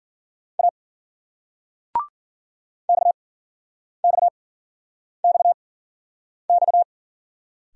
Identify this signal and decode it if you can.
Morse code, heard as a keyed tone, about 53 wpm, spaced as for 9 wpm, 700 Hz. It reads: A XXXX